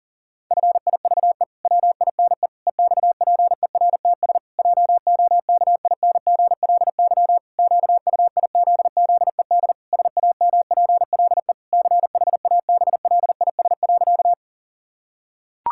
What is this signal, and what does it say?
Morse code, heard as a keyed tone, about 37 wpm, 700 Hz. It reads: FIVE WINE EXPERTS JOKINGLY QUIZZED SAMPLE CHABLIS.
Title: Pangram Challenge v2 - 40wpm